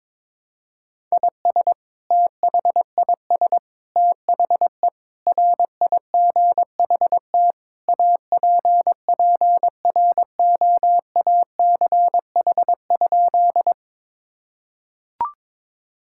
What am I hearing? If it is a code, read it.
Morse code, heard as a keyed tone, about 22 wpm, 700 Hz. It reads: IS THIS THE RIGHT APPROACH?
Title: Sentences from Top 1000 Words - Review Part 6 - 22wpm